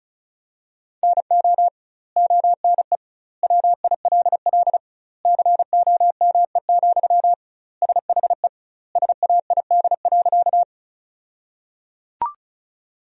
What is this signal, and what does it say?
Morse code, heard as a keyed tone, about 34 wpm, 700 Hz. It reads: NO ONE WILL COME, SHE SAID.
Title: 2x Word-Spacing - Sentences from Top 100 Words - 35wpm